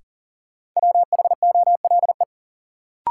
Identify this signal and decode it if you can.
Morse code, heard as a keyed tone, about 40 wpm, 700 Hz. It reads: WHOLE
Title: Sentences from Top 400 Words - New Word Drill - 40wpm